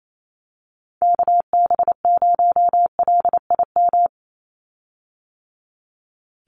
Morse code, heard as a keyed tone, about 28 wpm, 700 Hz. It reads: KB0LIM
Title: USA Call Signs 28wpm